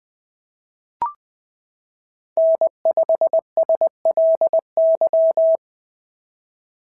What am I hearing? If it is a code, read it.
Morse code, heard as a keyed tone, about 19 wpm, 650 Hz. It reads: N5SLY
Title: POTA QSO Practice - 20wpm